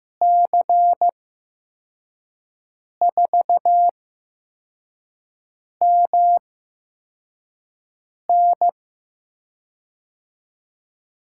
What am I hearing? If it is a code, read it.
Morse code, heard as a keyed tone, about 16 wpm, spaced as for 4 wpm, 700 Hz. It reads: C4MN
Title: USA Call Signs - 5 Characters - 8x Character Spacing - 15wpm